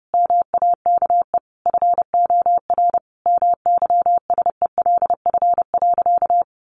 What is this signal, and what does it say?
Morse code, heard as a keyed tone, about 30 wpm, 700 Hz. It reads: MAKE FOR MYSELF.
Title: Sentences from Top 1300 Words 30wpm